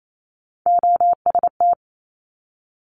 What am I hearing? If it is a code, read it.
Morse code, heard as a keyed tone, about 28 wpm, 700 Hz. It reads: OST
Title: Multi-letter Phonemes 28wpm